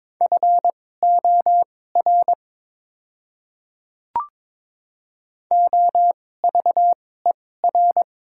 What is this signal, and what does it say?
Morse code, heard as a keyed tone, about 22 wpm, spaced as for 16 wpm, 690 Hz. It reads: FOR OVER
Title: 2x Character Spacing - Top 100 Words - 22wpm